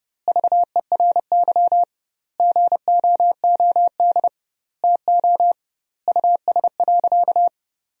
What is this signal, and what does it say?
Morse code, heard as a keyed tone, about 30 wpm, 700 Hz. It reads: VERY GOOD TO US.
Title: 2x Word-Spacing - Sentences from Top 200 Words - 30wpm